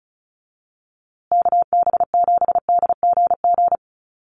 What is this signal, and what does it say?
Morse code, heard as a keyed tone, about 34 wpm, 700 Hz. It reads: KB7DGG
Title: USA Call Signs Repeated 2x 35wpm